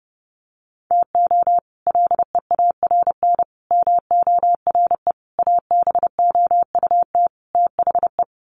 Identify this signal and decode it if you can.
Morse code, heard as a keyed tone, about 30 wpm, 700 Hz. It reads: TO LEARN MORE ABOUT THE
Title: Sets of 5 Words Encore2 30wpm